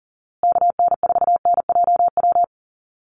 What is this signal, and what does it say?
Morse code, heard as a keyed tone, about 40 wpm, 700 Hz. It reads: KN4NJW
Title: USA Call Signs Repeated 2x 40wpm